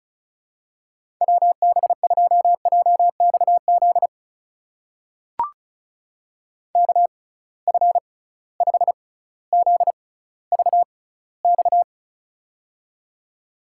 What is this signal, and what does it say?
Morse code, heard as a keyed tone, about 36 wpm, spaced as for 18 wpm, 700 Hz. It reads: WB2JXZ KF5ZVX
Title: USA Call Signs - 6x Character Spacing - 35wpm